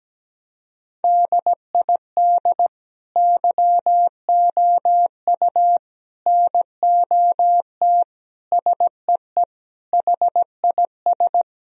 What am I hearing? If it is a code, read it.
Morse code, heard as a keyed tone, about 17 wpm, 700 Hz. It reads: DID YOU NOT SEE HIS
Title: Sentences from Top 1000 Words - Review Part 2 - 17wpm